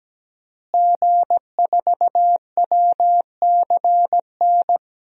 Morse code, heard as a keyed tone, about 17 wpm, 700 Hz. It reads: G4WCN
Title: DXCC-100 Call Signs - 5 Characters - Repeated 2x - 17wpm